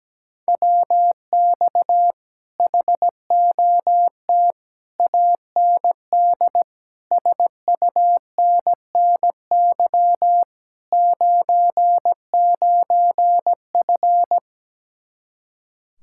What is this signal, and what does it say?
Morse code, heard as a keyed tone, about 17 wpm, 700 Hz. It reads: WX HOT AND SUNNY 99F
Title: Speed-Racing Common QSO Phrases - 17wpm